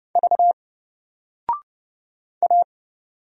Morse code, heard as a keyed tone, about 30 wpm, 700 Hz. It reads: V A
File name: Single-Letter-Number-30wpm